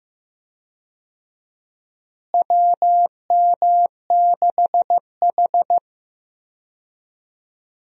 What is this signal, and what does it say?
Morse code, heard as a keyed tone, about 16 wpm, 700 Hz. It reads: WM6H
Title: USA Call Signs - 4 Characters - Repeated 2x - 15wpm